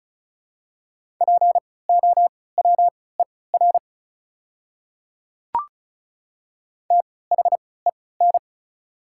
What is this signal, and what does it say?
Morse code, heard as a keyed tone, about 36 wpm, spaced as for 20 wpm, 690 Hz. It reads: POWER THEN
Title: 3x Character Spacing - Top 500 Words - 35wpm